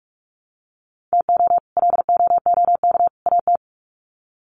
Morse code, heard as a keyed tone, about 44 wpm, 700 Hz. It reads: TO LOOK AT
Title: Sets of 3 Words 45wpm